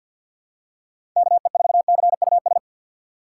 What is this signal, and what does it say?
Morse code, heard as a keyed tone, about 47 wpm, 700 Hz. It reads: KE4CUS
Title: USA Call Signs - 6 Characters - 6x Character Spacing - 50wpm